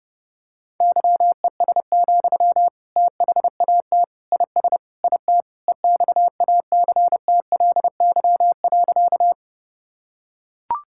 Morse code, heard as a keyed tone, about 32 wpm, 700 Hz. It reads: YES, THAT IS IT EXACTLY.
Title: Sentences from Top 1000 Words - Review Part 6 - 30wpm